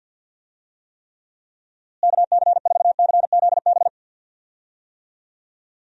Morse code, heard as a keyed tone, about 52 wpm, 700 Hz. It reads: KC4CZB
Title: USA Call Signs - 6 Characters - Repeated 2x - 50wpm